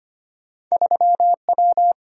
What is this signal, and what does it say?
Morse code, heard as a keyed tone, about 25 wpm, 700 Hz. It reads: 3W
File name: Two Letter-Number Combinations - 25wpm